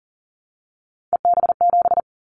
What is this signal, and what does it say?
Morse code, heard as a keyed tone, about 42 wpm, 700 Hz. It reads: EB7
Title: Callsign Prefixes for Top 25 DXCC - 40wpm